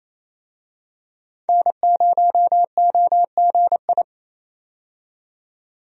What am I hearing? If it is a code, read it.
Morse code, heard as a keyed tone, about 28 wpm, 700 Hz. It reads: N0OGI
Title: USA Call Signs - 5 Characters - 28wpm